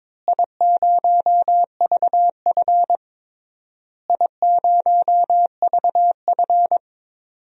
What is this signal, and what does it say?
Morse code, heard as a keyed tone, about 22 wpm, 700 Hz. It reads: I0VF I0VF
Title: DXCC-100 Call Signs - 4 Characters - Repeated 3x - 22wpm